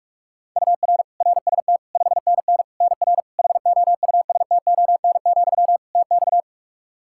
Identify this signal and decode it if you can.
Morse code, heard as a keyed tone, about 46 wpm, 700 Hz. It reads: UR RST 5NN NR HOUSTON, TX
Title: K1USN SST QSO Practice 45wpm